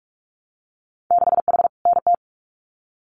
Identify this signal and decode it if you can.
Morse code, heard as a keyed tone, about 44 wpm, 700 Hz. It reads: 6H NT
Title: Winter Field Day Exchanges Encore 45wpm